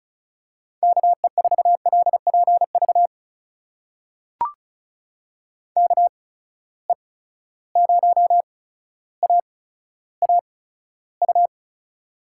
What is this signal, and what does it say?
Morse code, heard as a keyed tone, about 34 wpm, spaced as for 15 wpm, 700 Hz. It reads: KE4LPV KE0AAU
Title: USA Call Signs - 6 Characters - 8x Character Spacing - 35wpm